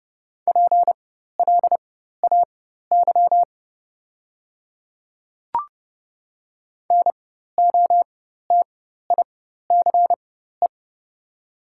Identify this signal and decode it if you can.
Morse code, heard as a keyed tone, about 30 wpm, spaced as for 14 wpm, 690 Hz. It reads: PLAY NOTICE
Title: 4x Character Spacing - Top 500 Words - 30wpm